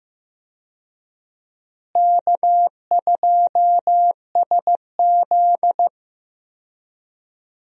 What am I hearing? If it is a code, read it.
Morse code, heard as a keyed tone, about 15 wpm, 700 Hz. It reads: K2SZ
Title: USA Call Signs - 4 Characters - Repeated 2x - 15wpm